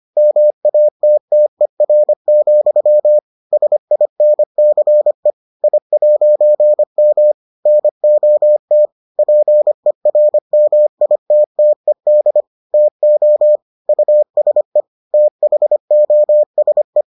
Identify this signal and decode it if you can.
Morse code, heard as a keyed tone, about 25 wpm, 600 Hz. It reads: MATTER, SINCE I'M NOT PERMITTED TO USE THOSE